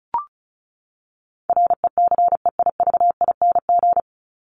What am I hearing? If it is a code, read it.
Morse code, heard as a keyed tone, about 34 wpm, 700 Hz. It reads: RECEIVING
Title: ING Suffix - Top 100 Words with Base Words 35wpm